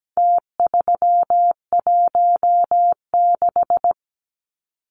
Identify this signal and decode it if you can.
Morse code, heard as a keyed tone, about 17 wpm, 700 Hz. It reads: T316
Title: Numbers - Contest Style - 4-Digit 17wpm